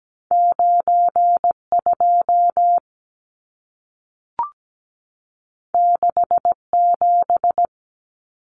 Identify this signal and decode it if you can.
Morse code, heard as a keyed tone, about 17 wpm, 700 Hz. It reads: 92 67
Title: Numbers - 2-Digit 17wpm